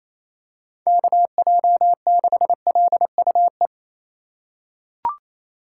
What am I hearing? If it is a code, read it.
Morse code, heard as a keyed tone, about 27 wpm, 700 Hz. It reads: KJ6LUE
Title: USA Call Signs - 2x Character Spacing - 28wpm